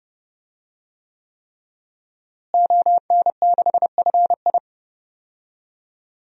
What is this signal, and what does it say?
Morse code, heard as a keyed tone, about 32 wpm, 700 Hz. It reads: ON6FI
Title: DXCC-100 Call Signs - 5 Characters - Repeated 3x - 30wpm